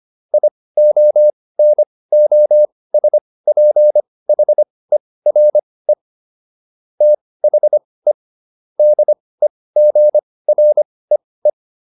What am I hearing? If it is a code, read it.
Morse code, heard as a keyed tone, about 25 wpm, spaced as for 18 wpm, 600 Hz. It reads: IONOSPHERE THE DEGREE